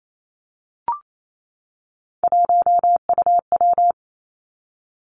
Morse code, heard as a keyed tone, about 28 wpm, 700 Hz. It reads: 1UW
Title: Three Challenging Letter-Number Combinations 28wpm